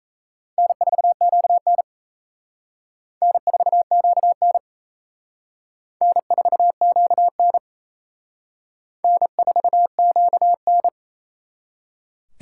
Morse code, heard as a keyed tone, about 35 wpm, 700 Hz. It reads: N4QN N4QN N4QN N4QN
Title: Speed Racing - USA 1x2 Call Signs 28wpm